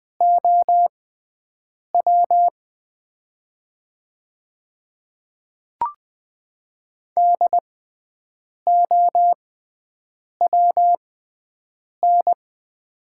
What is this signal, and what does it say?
Morse code, heard as a keyed tone, about 20 wpm, spaced as for 7 wpm, 700 Hz. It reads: OW DOWN
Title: 6x Character Spacing - Top 200 Words - 20wpm